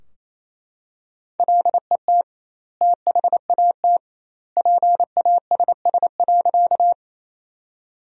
Morse code, heard as a keyed tone, about 28 wpm, 700 Hz. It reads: LET THAT PASS.
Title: 2x Word-Spacing - Sentences from Top 400 Words - 28wpm